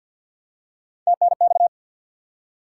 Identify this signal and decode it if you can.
Morse code, heard as a keyed tone, about 52 wpm, 700 Hz. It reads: TNX
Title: Top 90 QSO Elements 50wpm